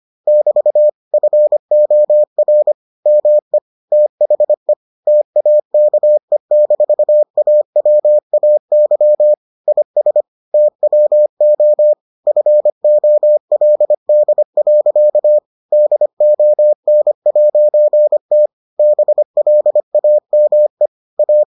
Morse code, heard as a keyed tone, about 25 wpm, 600 Hz. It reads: <BT> FOR ME THE TAKE-AWAY IS TWO FOLD. DON'T BLAME A